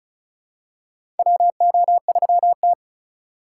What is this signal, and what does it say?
Morse code, heard as a keyed tone, about 36 wpm, 700 Hz. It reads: WO3T
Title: USA Call Signs - 4 Characters - 2x Character Spacing - 35wpm